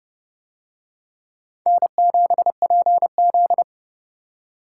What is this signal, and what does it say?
Morse code, heard as a keyed tone, about 29 wpm, 700 Hz. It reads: N7PZ